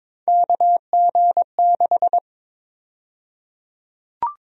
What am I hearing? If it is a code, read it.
Morse code, heard as a keyed tone, about 21 wpm, 700 Hz. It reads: KG6
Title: USA Call Sign Prefixes - Repeated 2x - 22wpm